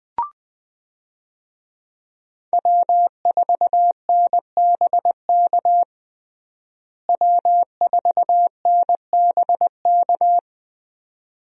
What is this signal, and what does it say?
Morse code, heard as a keyed tone, about 20 wpm, 700 Hz. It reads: W4NBK W4NBK
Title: USA Call Signs - 5 Characters - Repeated 2x - 20wpm